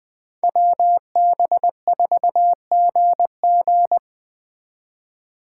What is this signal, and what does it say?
Morse code, heard as a keyed tone, about 21 wpm, 700 Hz. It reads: WB4GG
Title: Speed Racing - USA 2x2 Call Signs 17wpm